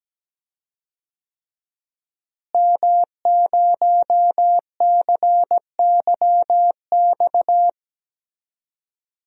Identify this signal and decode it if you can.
Morse code, heard as a keyed tone, about 17 wpm, 700 Hz. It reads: M0CYX
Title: DXCC-100 Call Signs - 5 Characters - Repeated 3x - 17wpm